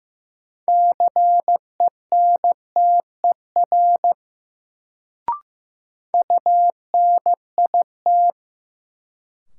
Morse code, heard as a keyed tone, about 16 wpm, 700 Hz. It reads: CENTER UNIT
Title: Sentences from Top 1000 Words - Review Part 2 - New Word Drill - 15wpm